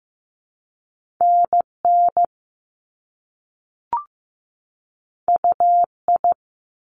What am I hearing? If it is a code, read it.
Morse code, heard as a keyed tone, about 15 wpm, 700 Hz. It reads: NN UI
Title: Top Two Letter Combinations in Words 15wpm